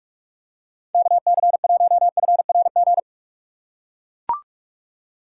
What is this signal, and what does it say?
Morse code, heard as a keyed tone, about 41 wpm, 700 Hz. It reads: KC1FRG
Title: USA Call Signs - 6 Characters - 4x Character Spacing - 45wpm